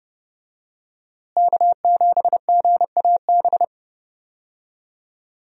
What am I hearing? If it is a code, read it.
Morse code, heard as a keyed tone, about 30 wpm, 700 Hz. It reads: K7GAB
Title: USA Call Signs - 5 Characters - Repeated 2x - 30wpm